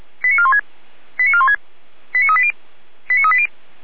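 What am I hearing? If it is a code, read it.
Five-tone selective call (ZVEI), heard as quick runs of five tones, tones of about 70 ms. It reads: 89416 89416 89380 89380
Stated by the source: ZVEI five-tone call received by SDR, archived by Feuerwehr Orlamünde (fire brigade).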